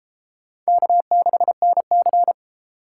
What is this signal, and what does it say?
Morse code, heard as a keyed tone, about 29 wpm, 700 Hz. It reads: K6NC